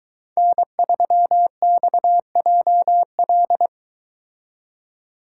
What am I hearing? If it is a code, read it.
Morse code, heard as a keyed tone, about 22 wpm, 700 Hz. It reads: N3XJL